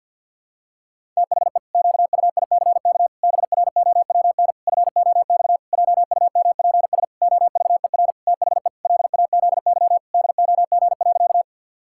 Morse code, heard as a keyed tone, about 52 wpm, 700 Hz. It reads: THE QUICK BROWN FOX JUMPS OVER THE LAZY DOG.
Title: Pangram Challenge v2 - 50wpm